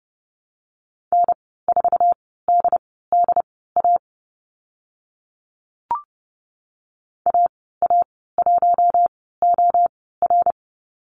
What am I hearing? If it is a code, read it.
Morse code, heard as a keyed tone, about 30 wpm, spaced as for 17 wpm, 700 Hz. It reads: N4DDA AA1OR
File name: Lesson 14g - Number 2 - Call Signs 30wpm